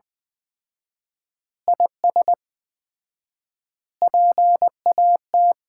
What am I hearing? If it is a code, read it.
Morse code, heard as a keyed tone, about 20 wpm, 700 Hz. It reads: IS PAT
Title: Common QSO Phrases - 4x Word-Spacing - 20wpm